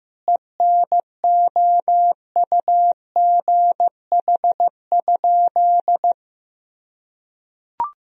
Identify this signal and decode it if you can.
Morse code, heard as a keyed tone, about 15 wpm, 700 Hz. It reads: ENOUGH?
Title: Sentences from Top 1000 Words - Review Part 3 - 15wpm